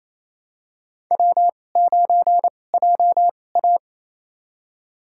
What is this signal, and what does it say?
Morse code, heard as a keyed tone, about 28 wpm, spaced as for 20 wpm, 700 Hz. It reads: W9JA
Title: USA Call Signs - 4 Characters - 2x Character Spacing - 28wpm